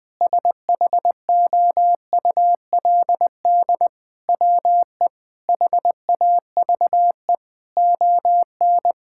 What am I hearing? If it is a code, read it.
Morse code, heard as a keyed tone, about 20 wpm, 700 Hz. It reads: SHOULD WE HAVE ON
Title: Sentences from Top 1000 Words - Review Part 5 - 20wpm